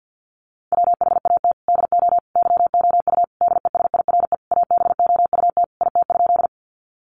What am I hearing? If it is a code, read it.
Morse code, heard as a keyed tone, about 50 wpm, 700 Hz. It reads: WHAT DO YOU DESIRE ABOUT IT?
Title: Sentences from Top 1100 Words 50wpm